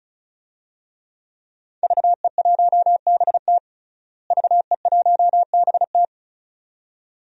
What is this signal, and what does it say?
Morse code, heard as a keyed tone, about 34 wpm, 700 Hz. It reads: VE1BT VE1BT
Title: DXCC-100 Call Signs - 5 Characters - Repeated 2x - 35wpm